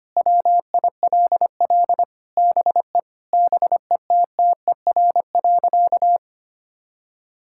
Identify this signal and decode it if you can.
Morse code, heard as a keyed tone, about 25 wpm, 700 Hz. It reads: WILL BE BETTER.
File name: practice / 25wpm / Sentences with 2-Word Phrases - 25wpm